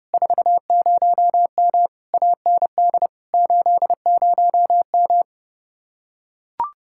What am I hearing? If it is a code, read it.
Morse code, heard as a keyed tone, about 29 wpm, 700 Hz. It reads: 40M AND 80M